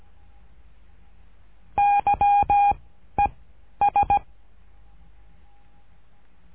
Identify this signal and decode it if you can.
Morse code, heard as a keyed tone, about 17 wpm, 810 Hz. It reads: Y E S
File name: cursos / CURSO DE TELEGRAFIA / Telegrafia - Trilha 13